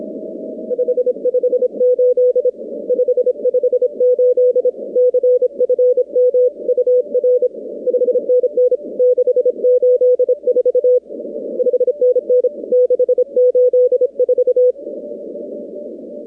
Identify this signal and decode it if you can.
Morse code, heard as a keyed tone, about 26 wpm, 500 Hz. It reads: 558 558 CFM UR 5NN 684 5NN 684
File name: contests / 2019-cw-wpx-kt5j-k5pi- -2330 0-45